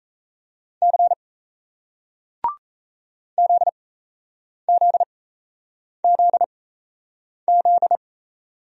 Morse code, heard as a keyed tone, about 35 wpm, spaced as for 8 wpm, 700 Hz. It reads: C ZZZZ